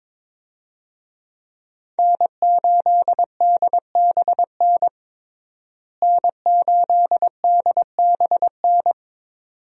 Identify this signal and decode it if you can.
Morse code, heard as a keyed tone, about 22 wpm, 700 Hz. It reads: N8DBN N8DBN
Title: USA Call Signs - 5 Characters - Repeated 2x - 22wpm